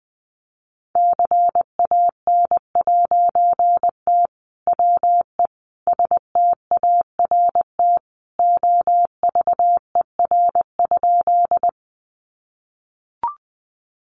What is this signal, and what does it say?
Morse code, heard as a keyed tone, about 20 wpm, 700 Hz. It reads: CAN'T WE START OVER?
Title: A Sound Of Thunder 20wpm